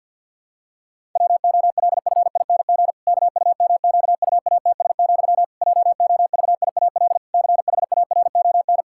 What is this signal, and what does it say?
Morse code, heard as a keyed tone, about 51 wpm, 700 Hz. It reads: WOLFING KUMQUATS, JOVIAL CHARON
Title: Pangram Challenge v2 - 50wpm